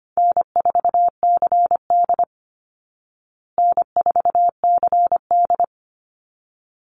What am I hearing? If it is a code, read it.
Morse code, heard as a keyed tone, about 25 wpm, 700 Hz. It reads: N4CD N4CD
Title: Call Signs Repeated 3x - 25wpm